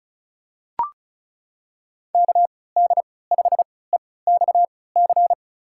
Morse code, heard as a keyed tone, about 36 wpm, spaced as for 20 wpm, 700 Hz. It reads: KD5EXC
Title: USA Call Signs - 3x Character Spacing - 35wpm